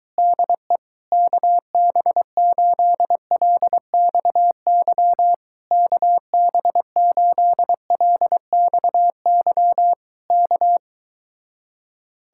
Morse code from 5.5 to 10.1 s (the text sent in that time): KB8LXY